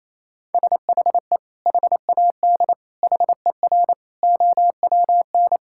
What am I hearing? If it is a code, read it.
Morse code, heard as a keyed tone, about 28 wpm, 700 Hz. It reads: SHE HAD HER OWN
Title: Sentences from Top 1000 Words - Review Part 5 - 28wpm